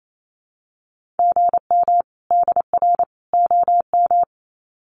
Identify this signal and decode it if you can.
Morse code, heard as a keyed tone, about 28 wpm, 700 Hz. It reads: GM DR OM
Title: Common QSO Phrases 28wpm